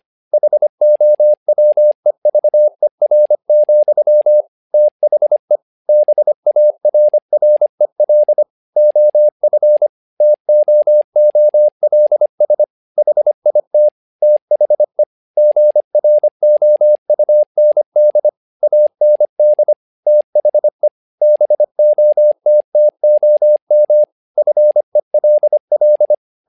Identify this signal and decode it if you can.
Morse code, heard as a keyed tone, about 25 wpm, 600 Hz. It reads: HOWEVER, THE BARREL OF TOOLS HIT THE GROUND AND THE BOTTOM FELL
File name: learning center / tower trouble / tower trouble 25wpm